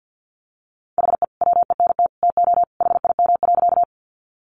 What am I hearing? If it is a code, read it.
Morse code, heard as a keyed tone, about 50 wpm, 700 Hz. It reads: HE WENT TO HIM.
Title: Sentences from Top 200 Words 50wpm